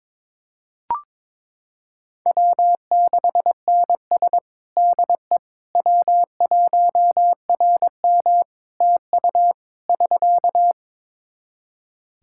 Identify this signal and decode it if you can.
Morse code, heard as a keyed tone, about 22 wpm, 700 Hz. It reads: W6NS DE W1RM TU <SK>
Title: Practice QSO - Exchange 4 - Ending - 22wpm